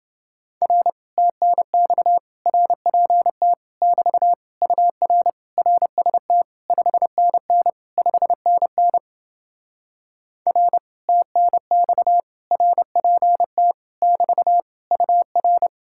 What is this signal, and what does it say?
Morse code, heard as a keyed tone, about 29 wpm, 700 Hz. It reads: R TNX RPT <BT> UR RST 5NN 5NN R TNX RPT <BT> UR